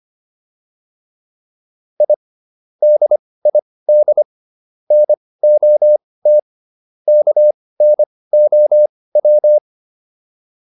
Morse code, heard as a keyed tone, about 25 wpm, spaced as for 18 wpm, 600 Hz. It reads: I DID NOT KNOW